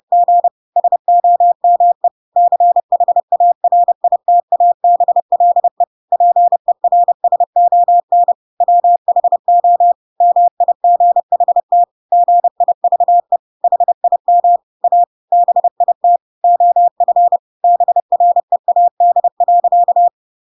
Morse code, heard as a keyed tone, about 30 wpm, 700 Hz. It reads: G SOME CHARITABLE PERSON WHO MIGHT GIVE HIM A BIT OF BREAD.
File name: Pinocchio / Pinocchio ch5 30 30 ik1hge